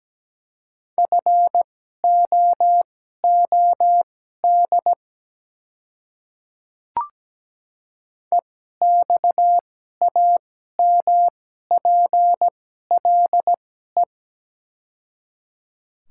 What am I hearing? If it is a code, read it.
Morse code, heard as a keyed tone, about 17 wpm, spaced as for 12 wpm, 700 Hz. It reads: FOOD EXAMPLE
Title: 2x Character Spacing - Top 300 Words - 17wpm